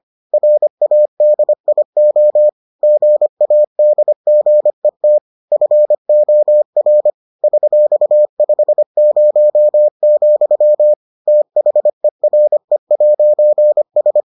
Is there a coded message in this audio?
Yes: Morse code from 0.3 to 14.2 s, heard as a keyed tone, about 25 wpm, 600 Hz. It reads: RADIO GADGET FOR $50, THERE'S